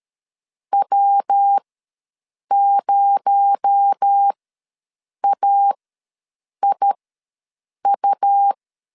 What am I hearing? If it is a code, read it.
Morse code, heard as a keyed tone, about 13 wpm, spaced as for 7 wpm, 800 Hz. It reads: W0AIU
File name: LEZIONI / 6 PKX0 / 6 PKX0 ws 12wpm 3448